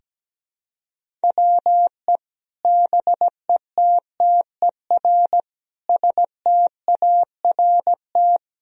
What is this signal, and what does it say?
Morse code, heard as a keyed tone, about 17 wpm, 700 Hz. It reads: WE BETTER START
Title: Sentences from Top 1000 Words - Review Part 1 - 17wpm